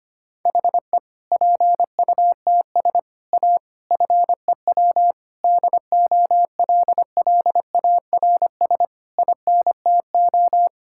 Morse code, heard as a keyed tone, about 25 wpm, 700 Hz. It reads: HE PUTS A FEW DOLLARS INTO
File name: practice / 25wpm / Difficult Sentences with Hints - 25wpm